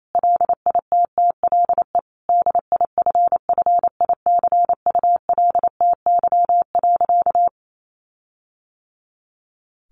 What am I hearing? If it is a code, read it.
Morse code, heard as a keyed tone, about 28 wpm, 700 Hz. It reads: LITTLE DIFFICULTY.
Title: Sentences from Top 1500 Words 28wpm